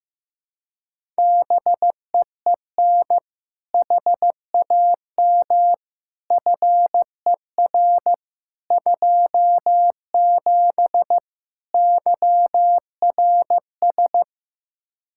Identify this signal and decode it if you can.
Morse code, heard as a keyed tone, about 15 wpm, 700 Hz. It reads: BEEN HAM FER 27 YRS
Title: Practice QSO - Exchange 3 - More Info - 15wpm